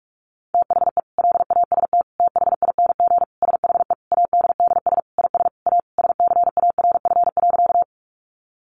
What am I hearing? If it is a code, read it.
Morse code, heard as a keyed tone, about 44 wpm, 700 Hz. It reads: THE LAST THING SHE ADDS IS A SCARF.
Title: Easy Phrases 45wpm